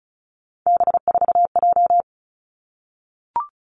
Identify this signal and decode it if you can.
Morse code, heard as a keyed tone, about 36 wpm, 700 Hz. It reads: B4J
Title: Three Challenging Letter-Number Combinations 35wpm